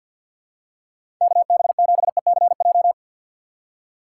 Morse code, heard as a keyed tone, about 46 wpm, 700 Hz. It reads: KB7ECJ
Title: USA Call Signs - 4x Character Spacing - 50wpm